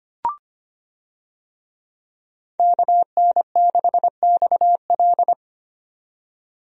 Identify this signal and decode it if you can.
Morse code, heard as a keyed tone, about 25 wpm, 700 Hz. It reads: KN6XL
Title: USA Call Signs - 5 Characters - 25wpm